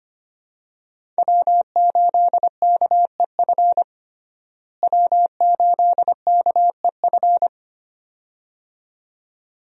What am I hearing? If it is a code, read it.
Morse code, heard as a keyed tone, about 25 wpm, 700 Hz. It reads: W8KEF W8KEF
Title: USA Call Signs - 5 Characters - Repeated 2x - 25wpm